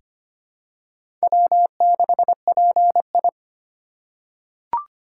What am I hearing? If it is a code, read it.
Morse code, heard as a keyed tone, about 25 wpm, 700 Hz. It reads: W6PI
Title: USA Call Signs - 4 Characters - 3x Character Spacing - 25wpm